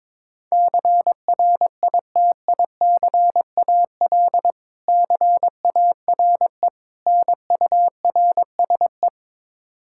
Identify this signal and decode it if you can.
Morse code, heard as a keyed tone, about 22 wpm, 700 Hz. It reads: CRITICAL CARE NURSE
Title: Practice QSO - Exchange 3 - More Info - 22wpm